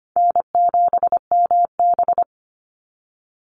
Morse code, heard as a keyed tone, about 25 wpm, 700 Hz. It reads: N7MB
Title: Call Signs - 25wpm